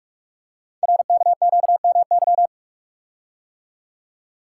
Morse code, heard as a keyed tone, about 44 wpm, 700 Hz. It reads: RKQMY
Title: Code Groups - 5 Letter:Numbers - 45wpm